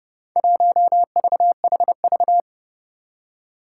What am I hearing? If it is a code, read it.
Morse code, heard as a keyed tone, about 30 wpm, 700 Hz. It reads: 1VHV
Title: Four Challenging Letter-Number Combinations 30wpm